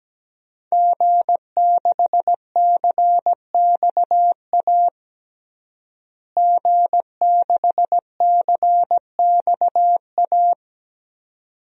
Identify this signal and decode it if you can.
Morse code, heard as a keyed tone, about 17 wpm, 700 Hz. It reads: G6CXA G6CXA
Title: DXCC-100 Call Signs - 5 Characters - Repeated 2x - 17wpm